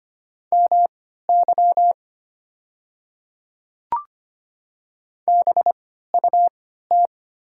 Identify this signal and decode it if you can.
Morse code, heard as a keyed tone, about 25 wpm, spaced as for 14 wpm, 700 Hz. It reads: MY BUT
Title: 3x Character Spacing - Top 100 Words - 25wpm